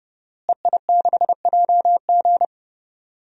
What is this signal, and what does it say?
Morse code, heard as a keyed tone, about 30 wpm, 700 Hz. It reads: EI6JG
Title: DXCC-100 Call Signs - 5 Characters - Repeated 3x - 30wpm